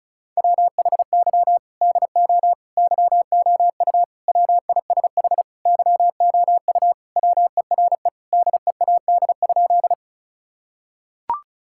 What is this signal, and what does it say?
Morse code, heard as a keyed tone, about 34 wpm, 700 Hz. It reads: WHY DO YOU WISH YOU WERE DEAD?
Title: Sentences from Top 1600 Words 35wpm